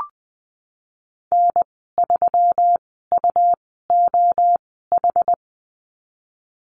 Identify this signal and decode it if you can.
Morse code, heard as a keyed tone, about 20 wpm, spaced as for 14 wpm, 700 Hz. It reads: N3UOH
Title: 128 Lesson 21g - Number 6 - Call Signs 20wpm